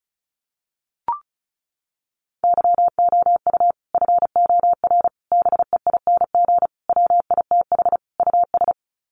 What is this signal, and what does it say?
Morse code, heard as a keyed tone, about 34 wpm, 700 Hz. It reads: YOU FOR BEING WITH US
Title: Sets of 5 Words 35wpm